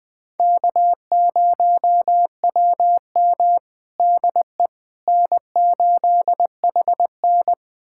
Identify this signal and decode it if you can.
Morse code, heard as a keyed tone, about 20 wpm, 700 Hz. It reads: K0WM DE N8HN
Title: USA Callsign DE Callsign - 4 Characters - 20wpm